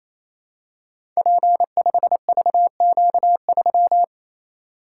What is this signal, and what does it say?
Morse code, heard as a keyed tone, about 27 wpm, 700 Hz. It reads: P5VQ3
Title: Code Groups - 5 Letter:Numbers - 28wpm